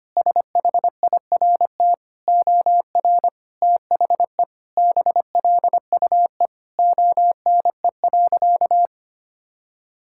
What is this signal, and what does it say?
Morse code, heard as a keyed tone, about 25 wpm, 700 Hz. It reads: SHIRT OR THE BLUE ONE.
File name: practice / 25wpm / Sentences with 2-Word Phrases - 25wpm